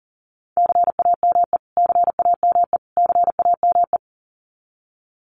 Morse code, heard as a keyed tone, about 40 wpm, 700 Hz. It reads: CAME CAME CAME
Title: Top 500 Words Repeated 40wpm